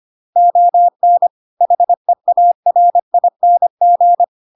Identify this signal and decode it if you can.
Morse code, heard as a keyed tone, about 25 wpm, 700 Hz. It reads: ON HEARING
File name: Pinocchio / Pinocchio ch2 25 25 ik1hge